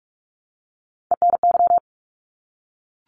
Morse code, heard as a keyed tone, about 44 wpm, 700 Hz. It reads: ENY
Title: ARRL Sections 45wpm